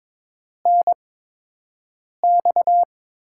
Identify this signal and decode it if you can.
Morse code, heard as a keyed tone, about 22 wpm, 700 Hz. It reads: N X
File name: Top 90 QSO Elements - 8x Character Spacing - 22wpm